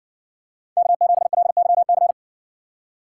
Morse code, heard as a keyed tone, about 59 wpm, 700 Hz. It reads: K7LYC